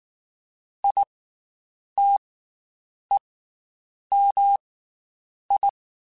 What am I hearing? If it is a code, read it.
Morse code, heard as a keyed tone, about 19 wpm, 800 Hz. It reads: I T E M I
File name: Special INGs / Special INGs-mp3s / Slow ing @18